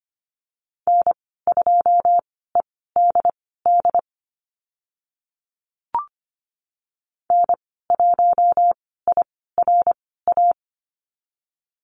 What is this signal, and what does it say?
Morse code, heard as a keyed tone, about 25 wpm, spaced as for 16 wpm, 690 Hz. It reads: N2EDD N1IRA
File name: Lesson 15g - Number 5 - Call Signs 25wpm